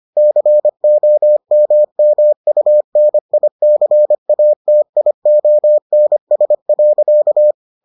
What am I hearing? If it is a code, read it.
Morse code, heard as a keyed tone, about 25 wpm, 600 Hz. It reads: COMMUNICATIONS.